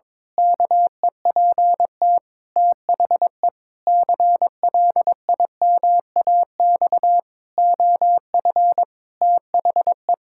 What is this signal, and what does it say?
Morse code, heard as a keyed tone, about 21 wpm, 700 Hz. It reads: KEPT THE CLIMAX OF THE
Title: Pangram Challenge v2 - 22wpm